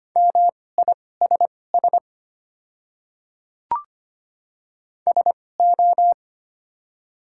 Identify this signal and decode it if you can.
Morse code, heard as a keyed tone, about 25 wpm, spaced as for 18 wpm, 700 Hz. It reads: MISS SO
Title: 2x Character Spacing - Top 500 Words - 25wpm